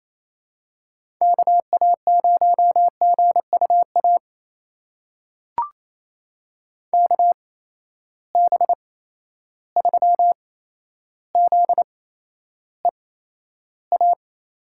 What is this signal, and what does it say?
Morse code, heard as a keyed tone, about 28 wpm, spaced as for 12 wpm, 700 Hz. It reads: KA0GUA KB3ZEA
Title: USA Call Signs - 6 Characters - 8x Character Spacing - 28wpm